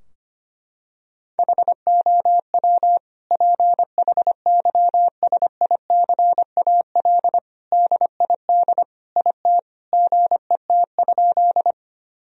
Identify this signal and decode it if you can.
Morse code, heard as a keyed tone, about 25 wpm, 700 Hz. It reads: HOW PHYSICAL DID IT GET?
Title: Sentences from Top 1000 Words - Review Part 4 - 25wpm